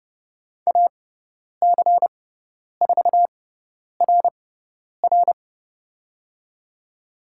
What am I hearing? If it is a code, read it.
Morse code, heard as a keyed tone, about 30 wpm, spaced as for 10 wpm, 700 Hz. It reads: AC4RR
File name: Lesson 16g - Letter c - Call Signs 30wpm